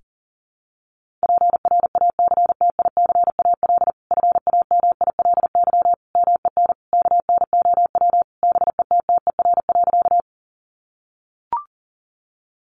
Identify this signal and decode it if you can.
Morse code, heard as a keyed tone, about 40 wpm, 700 Hz. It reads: PRACTICAL FAMILY MEN KNOW BETTER.
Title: Sentences from Top 1400 Words 40wpm